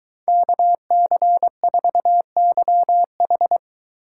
Morse code, heard as a keyed tone, about 23 wpm, 700 Hz. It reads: KC4YH